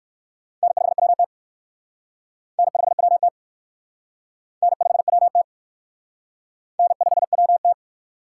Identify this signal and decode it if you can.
Morse code, heard as a keyed tone, about 56 wpm, 700 Hz. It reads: N5WT N5WT N5WT N5WT
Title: Speed Racing - USA 1x2 Call Signs 45wpm